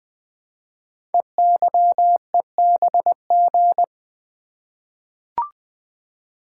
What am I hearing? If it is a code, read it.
Morse code, heard as a keyed tone, about 21 wpm, 700 Hz. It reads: EYEBG
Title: Code Groups - 5 Letters - 20wpm